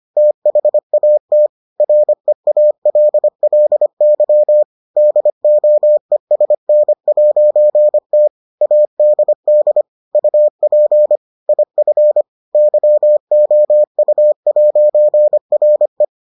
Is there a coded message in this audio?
Yes: Morse code from 0.2 to 16.0 s, heard as a keyed tone, about 25 wpm, 600 Hz. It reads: THAT REALLY DOESN'T ADD UP IF YOU'RE